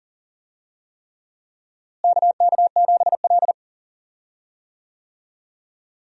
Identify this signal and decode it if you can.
Morse code, heard as a keyed tone, about 40 wpm, 700 Hz. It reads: KK7L
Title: USA Call Signs - 4 Characters - Repeated 3x - 40wpm